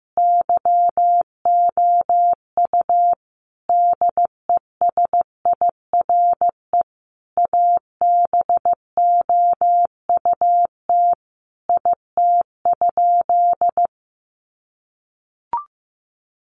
Morse code, heard as a keyed tone, about 15 wpm, 700 Hz. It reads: YOU DESIRE ABOUT IT?
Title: Sentences from Top 1100 Words 15wpm